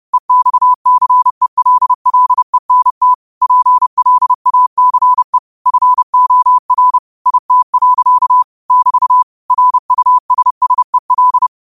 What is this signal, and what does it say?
Morse code, heard as a keyed tone, about 30 wpm, 1000 Hz. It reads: EXCELLENT PLACE FOR IT. <BT> RUSSEL